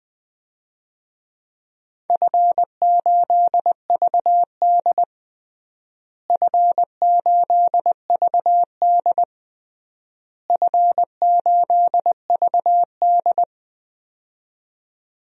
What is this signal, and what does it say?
Morse code, heard as a keyed tone, about 20 wpm, 700 Hz. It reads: F8VD F8VD F8VD
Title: DXCC-100 Call Signs - 4 Characters - Repeated 3x - 20wpm